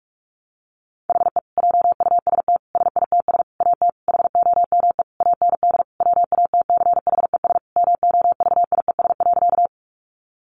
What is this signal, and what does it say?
Morse code, heard as a keyed tone, about 44 wpm, 700 Hz. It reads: HE JUST SITS AT HOME AND WATCHES MOVIES.
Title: Additional Easy Phrases 45wpm